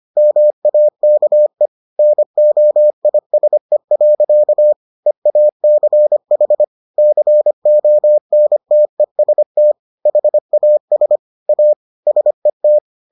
Morse code, heard as a keyed tone, about 25 wpm, 600 Hz. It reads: MAKE NOISE. EACH CONTEST HAS A SET